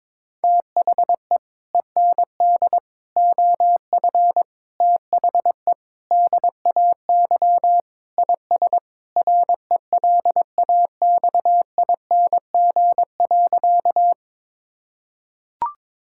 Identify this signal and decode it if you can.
Morse code, heard as a keyed tone, about 22 wpm, 700 Hz. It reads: THE END OF THE DAY IS RELAXING.
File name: practice / 22wpm / Sentences with 3-Word Phrases - 22wpm